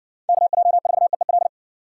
Morse code, heard as a keyed tone, about 60 wpm, 700 Hz. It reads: KJ4EEL